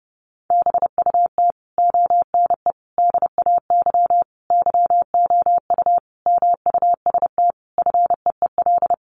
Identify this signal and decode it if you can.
Morse code, heard as a keyed tone, about 28 wpm, 700 Hz. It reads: BUT ONE DAY YOU MUST FEEL
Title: Sentences from Top 400 Words 30wpm